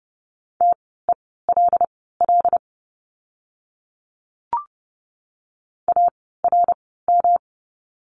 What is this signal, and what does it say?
Morse code, heard as a keyed tone, about 30 wpm, spaced as for 17 wpm, 700 Hz. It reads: TELL ARM
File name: course / 30:10wpm / 108 Lesson 18f - Letter m - Words 30wpm